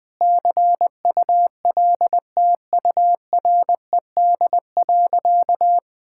Morse code, heard as a keyed tone, about 21 wpm, 700 Hz. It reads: CULTURED.
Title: All-Age Appropriate Jokes - 20wpm